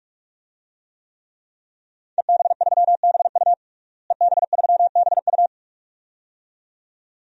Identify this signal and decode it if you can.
Morse code, heard as a keyed tone, about 46 wpm, 700 Hz. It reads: EB3BU EB3BU
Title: DXCC-100 Call Signs - 5 Characters - Repeated 2x - 45wpm